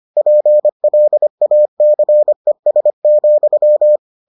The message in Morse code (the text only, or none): PLACES,